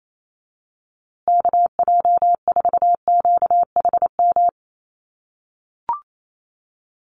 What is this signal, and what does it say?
Morse code, heard as a keyed tone, about 28 wpm, 700 Hz. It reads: KJ4QHM
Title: USA Call Signs Repeated 2x 28wpm